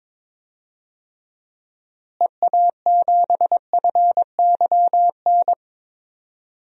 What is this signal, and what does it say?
Morse code, heard as a keyed tone, about 21 wpm, 700 Hz. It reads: EA7FYN
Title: DXCC-100 Call Signs - 6 Characters - Repeated 2x - 22wpm